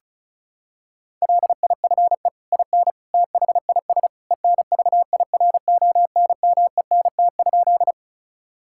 Morse code, heard as a keyed tone, about 34 wpm, 700 Hz. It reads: LIFE IN THIS ENVIRONMENT?
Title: Sentences from Top 1000 Words - Review Part 5 - 35wpm